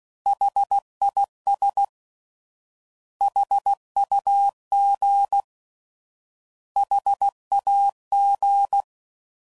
Morse code, heard as a keyed tone, about 16 wpm, 800 Hz. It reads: HIS HUG HAG
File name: Prefixes / By First Letter / Hs